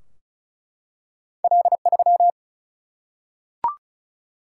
Morse code, heard as a keyed tone, about 36 wpm, 700 Hz. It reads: L3
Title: Single-Letter Single-Number 35wpm